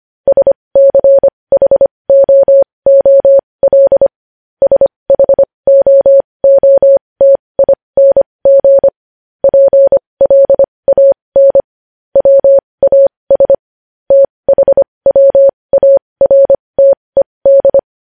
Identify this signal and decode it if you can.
Morse code, heard as a keyed tone, about 25 wpm, spaced as for 20 wpm, 550 Hz. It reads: SCHOOL SHOOTING PLAN WAS THWARTED
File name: podcast / News headlines-1x-25x20